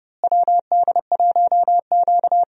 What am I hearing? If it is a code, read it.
Morse code, heard as a keyed tone, about 32 wpm, 700 Hz. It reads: WD1Q
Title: USA Call Signs - 4 Characters - Repeated 2x - 30wpm